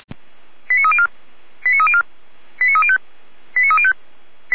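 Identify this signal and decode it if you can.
Five-tone selective call (ZVEI), heard as quick runs of five tones, tones of about 70 ms. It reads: 89304 89304 89305 89305
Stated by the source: ZVEI five-tone call 89220 received on an SDR by the Orlamünde fire brigade (feuerwehr-orlamuende).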